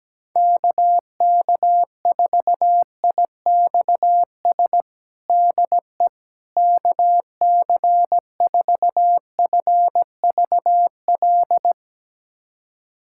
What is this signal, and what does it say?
Morse code, heard as a keyed tone, about 17 wpm, 700 Hz. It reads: KK4IXS DE KC4FVL
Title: USA Callsign DE Callsign - 6 Characters - 17wpm